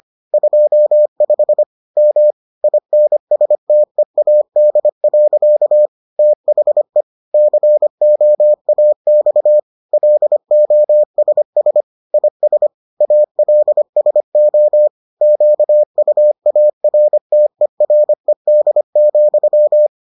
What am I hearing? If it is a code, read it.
Morse code, heard as a keyed tone, about 25 wpm, 600 Hz. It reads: 25 M INSTEAD. THE COAX LOSS IS ALSO QUARTERED,